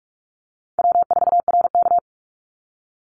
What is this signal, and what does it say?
Morse code, heard as a keyed tone, about 44 wpm, 690 Hz. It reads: W4RK
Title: NA SOTA Chaser Callsigns 45wpm